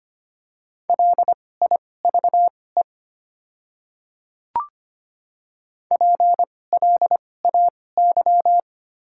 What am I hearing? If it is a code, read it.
Morse code, heard as a keyed tone, about 25 wpm, spaced as for 18 wpm, 700 Hz. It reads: LIVE PLAY
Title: 2x Character Spacing - Top 300 Words - 25wpm